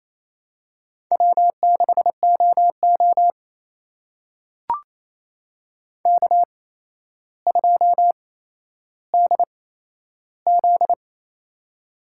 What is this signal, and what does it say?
Morse code, heard as a keyed tone, about 28 wpm, spaced as for 12 wpm, 700 Hz. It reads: W6OO K2DZ
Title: USA Call Signs - 4 Characters - 8x Character Spacing - 28wpm